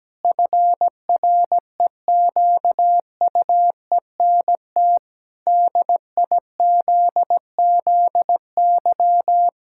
Morse code, heard as a keyed tone, about 17 wpm, 700 Hz. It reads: FREQUENT DIZZY
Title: Pangram Challenge v2 - 17wpm